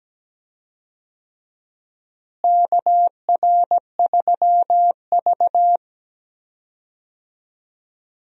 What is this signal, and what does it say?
Morse code, heard as a keyed tone, about 17 wpm, 700 Hz. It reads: KR3V
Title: USA Call Signs - 4 Characters - Repeated 2x - 17wpm